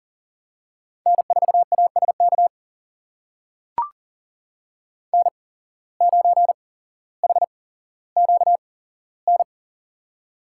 Morse code, heard as a keyed tone, about 40 wpm, spaced as for 17 wpm, 700 Hz. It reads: N4ASK N9HQN
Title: USA Call Signs - 5 Characters - 8x Character Spacing - 40wpm